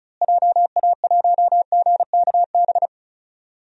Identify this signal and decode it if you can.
Morse code, heard as a keyed tone, about 36 wpm, 700 Hz. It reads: JA1GKB
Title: DXCC-100 Call Signs - 6 Characters - Repeated 2x - 35wpm